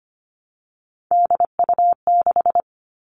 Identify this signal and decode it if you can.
Morse code, heard as a keyed tone, about 25 wpm, 700 Hz. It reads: DU6
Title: All Cut Numbers - 3-Digit 25wpm